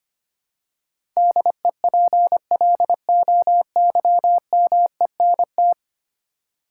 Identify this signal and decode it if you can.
Morse code, heard as a keyed tone, about 25 wpm, 700 Hz. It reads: DEPLOYMENT
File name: Ten Letter Words - 25wpm